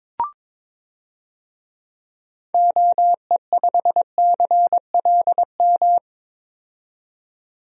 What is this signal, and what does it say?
Morse code, heard as a keyed tone, about 22 wpm, 700 Hz. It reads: OE5CLM
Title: DXCC-100 Call Signs - 6 Characters - 22wpm